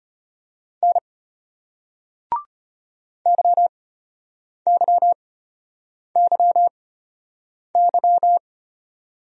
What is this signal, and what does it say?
Morse code, heard as a keyed tone, about 35 wpm, spaced as for 8 wpm, 700 Hz. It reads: N YYYY